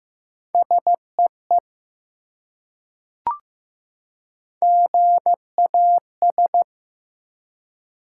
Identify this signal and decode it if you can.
Morse code, heard as a keyed tone, about 15 wpm, 700 Hz. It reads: SEE GAS
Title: Three Letter Words - 15wpm